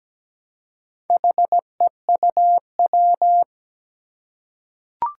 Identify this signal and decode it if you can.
Morse code, heard as a keyed tone, about 17 wpm, 700 Hz. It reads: HEUW
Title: Four Challenging Letter-Number Combinations 17wpm